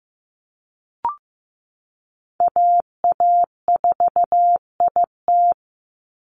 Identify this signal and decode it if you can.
Morse code, heard as a keyed tone, about 15 wpm, 700 Hz. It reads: AA4IT
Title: NA SOTA Chaser Callsigns 15wpm